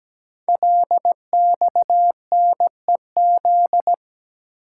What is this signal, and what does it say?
Morse code, heard as a keyed tone, about 17 wpm, 700 Hz. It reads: LXNEZ
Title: Code Groups - 5 Letters - 17wpm